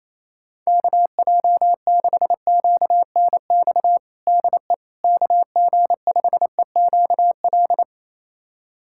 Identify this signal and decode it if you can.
Morse code, heard as a keyed tone, about 27 wpm, 700 Hz. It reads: KJ6QNX DE KG5EQL
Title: USA Callsign DE Callsign - 6 Characters - 28wpm